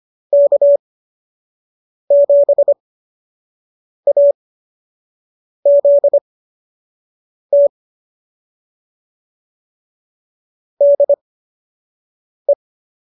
Morse code, heard as a keyed tone, about 25 wpm, spaced as for 6 wpm, 590 Hz. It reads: K7AZT DE